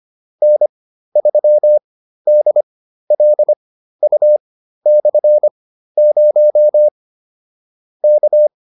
Morse code, heard as a keyed tone, about 25 wpm, spaced as for 13 wpm, 600 Hz. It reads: N3DLU/0 K